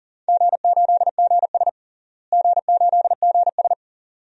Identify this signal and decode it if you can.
Morse code, heard as a keyed tone, about 39 wpm, 700 Hz. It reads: G8GS G8GS
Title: DXCC-100 Call Signs - 4 Characters - Repeated 2x - 40wpm